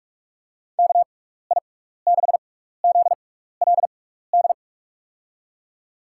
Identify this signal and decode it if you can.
Morse code, heard as a keyed tone, about 46 wpm, spaced as for 16 wpm, 700 Hz. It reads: KI6ZLD
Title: USA Call Signs - 6 Characters - 6x Character Spacing - 45wpm